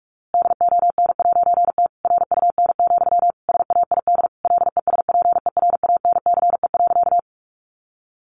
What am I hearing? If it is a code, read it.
Morse code, heard as a keyed tone, about 46 wpm, 700 Hz. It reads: DON'T RUN, SAID LESPERANCE.
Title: A Sound Of Thunder 45wpm